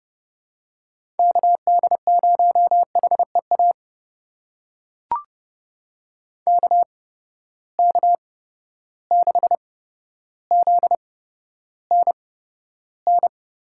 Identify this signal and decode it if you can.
Morse code, heard as a keyed tone, about 30 wpm, spaced as for 13 wpm, 700 Hz. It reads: KD0HEA KK6ZNN
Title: USA Call Signs - 6 Characters - 8x Character Spacing - 30wpm